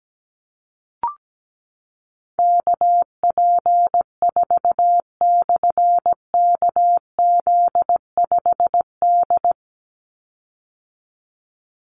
Morse code, heard as a keyed tone, about 17 wpm, 700 Hz. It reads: KP4/KZ5D
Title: Call Signs - 17wpm